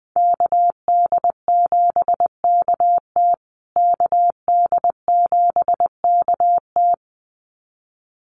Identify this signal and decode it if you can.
Morse code, heard as a keyed tone, about 20 wpm, 700 Hz. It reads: KD7KT KD7KT
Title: USA Call Signs Repeated 2x 20wpm